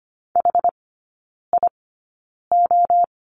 Morse code, heard as a keyed tone, about 25 wpm, spaced as for 9 wpm, 690 Hz. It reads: HIO
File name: Lesson 14e - Number 2 - Three Characters 25wpm